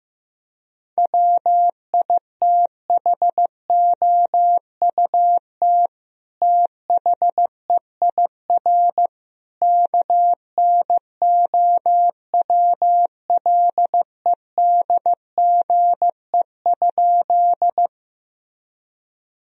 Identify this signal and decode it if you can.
Morse code, heard as a keyed tone, about 15 wpm, 700 Hz. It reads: WITHOUT THEIR KNOWLEDGE?
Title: Sentences from Top 1000 Words - Review Part 3 - 15wpm